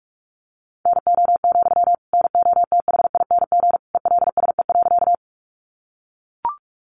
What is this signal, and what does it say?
Morse code, heard as a keyed tone, about 44 wpm, 700 Hz. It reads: NO, NOTHING ELSE.
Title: Sentences from Top 800 Words 45wpm